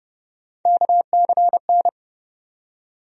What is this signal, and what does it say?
Morse code, heard as a keyed tone, about 31 wpm, 700 Hz. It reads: KCN
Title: USA Call Sign Postfixes - 3 Letters 30wpm